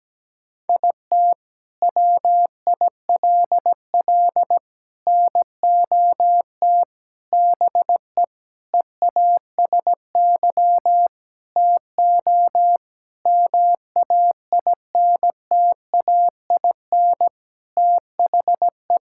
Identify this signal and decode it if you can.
Morse code, heard as a keyed tone, about 17 wpm, 700 Hz. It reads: IT WILL NOT BE EASY TO MAINTAIN THE
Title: Sentences from Top 1000 Words - Review Part 5 - 17wpm